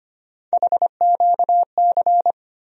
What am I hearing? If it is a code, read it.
Morse code, heard as a keyed tone, about 25 wpm, 700 Hz. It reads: HQC
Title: USA Call Sign Postfixes - 3 Letters 25wpm